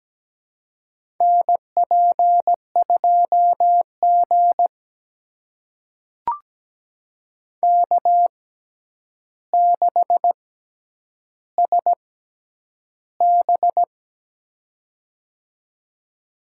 Morse code, heard as a keyed tone, about 17 wpm, spaced as for 9 wpm, 700 Hz. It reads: NP2G K6SB
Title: USA Call Signs - 4 Characters - 6x Character Spacing - 17wpm